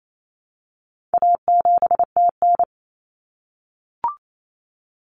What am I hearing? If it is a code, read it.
Morse code, heard as a keyed tone, about 28 wpm, 700 Hz. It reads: A7TN
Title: Common Cut Numbers - 4-Digit 28wpm